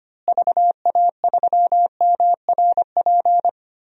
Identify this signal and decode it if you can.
Morse code, heard as a keyed tone, about 25 wpm, 700 Hz. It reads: VA3MRP
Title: DXCC-100 Call Signs - 6 Characters - Repeated 2x - 25wpm